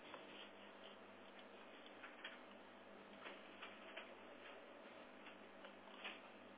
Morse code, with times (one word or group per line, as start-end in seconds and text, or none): none